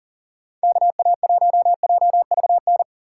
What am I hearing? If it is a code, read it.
Morse code, heard as a keyed tone, about 40 wpm, 700 Hz. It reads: KA1JVN
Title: USA Call Signs - 6 Characters - 2x Character Spacing - 40wpm